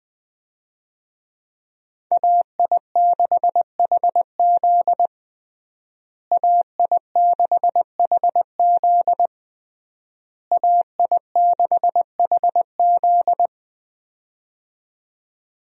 Morse code, heard as a keyed tone, about 21 wpm, 700 Hz. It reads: AI6HZ AI6HZ AI6HZ
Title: USA Call Signs - 5 Characters - Repeated 3x - 20wpm